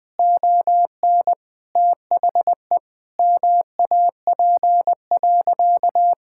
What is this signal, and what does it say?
Morse code, heard as a keyed tone, about 20 wpm, 700 Hz. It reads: ON THE MAP.
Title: Sentences from Top 1000 Words - Review Part 6 - 20wpm